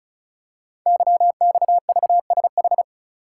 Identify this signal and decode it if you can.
Morse code, heard as a keyed tone, about 36 wpm, 700 Hz. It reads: YXVSH
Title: Code Groups - 5 Letter:Numbers - 35wpm